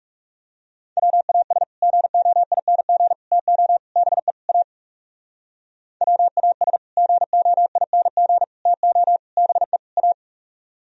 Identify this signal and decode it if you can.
Morse code, heard as a keyed tone, about 36 wpm, 700 Hz. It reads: WAS GOING TO BE A WAS GOING TO BE A
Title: Speed-Racing - Sets of 5 Words 30wpm